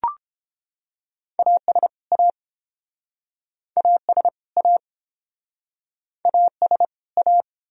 Morse code, heard as a keyed tone, about 29 wpm, 700 Hz. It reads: AS A AS A AS A